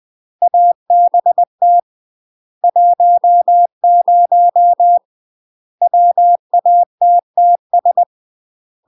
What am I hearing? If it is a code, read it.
Morse code, heard as a keyed tone, about 20 wpm, 700 Hz. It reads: ABT 10 WATTS
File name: UK MASTER QSOs TRAINING / 20WPM / UK MASTER QSOs 0004